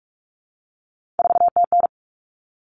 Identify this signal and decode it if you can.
Morse code, heard as a keyed tone, about 44 wpm, 700 Hz. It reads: 4TN
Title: All Cut Numbers - 3-Digit 45wpm